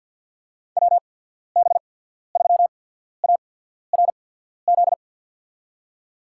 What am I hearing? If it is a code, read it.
Morse code, heard as a keyed tone, about 47 wpm, spaced as for 13 wpm, 700 Hz. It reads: WB3ARZ
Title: USA Call Signs - 6 Characters - 8x Character Spacing - 50wpm